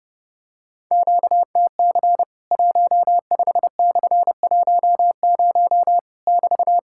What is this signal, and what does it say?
Morse code, heard as a keyed tone, about 31 wpm, 700 Hz. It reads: QTC 15/10 <BT>
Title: Worked All Europe DX Contest - QTC Exchange - 30wpm